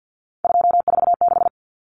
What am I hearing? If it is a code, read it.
Morse code, heard as a keyed tone, about 50 wpm, 700 Hz. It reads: 246
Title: Numbers - 3-Digit 50wpm